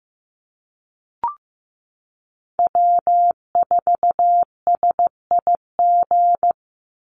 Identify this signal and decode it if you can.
Morse code, heard as a keyed tone, about 15 wpm, 700 Hz. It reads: W4SIG
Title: Call Signs Repeated 3x - 15wpm